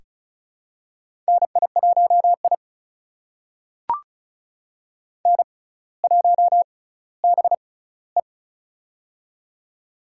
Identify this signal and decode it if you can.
Morse code, heard as a keyed tone, about 34 wpm, spaced as for 18 wpm, 700 Hz. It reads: NI1I N1BE
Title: USA Call Signs - 4 Characters - 6x Character Spacing - 35wpm